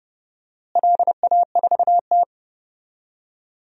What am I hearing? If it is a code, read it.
Morse code, heard as a keyed tone, about 31 wpm, 700 Hz. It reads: LA4T
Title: DXCC-100 Call Signs - 4 Characters - Repeated 3x - 30wpm